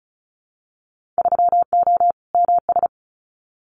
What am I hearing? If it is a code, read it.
Morse code, heard as a keyed tone, about 34 wpm, 700 Hz. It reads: 3O MS
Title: Winter Field Day Exchanges Encore 35wpm